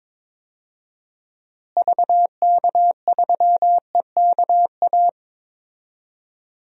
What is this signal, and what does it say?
Morse code, heard as a keyed tone, about 21 wpm, 700 Hz. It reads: VK3EKA
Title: DXCC-100 Call Signs - 6 Characters - Repeated 2x - 22wpm